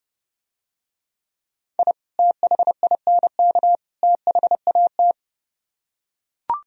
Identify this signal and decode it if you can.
Morse code, heard as a keyed tone, about 32 wpm, 700 Hz. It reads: I THINK THAT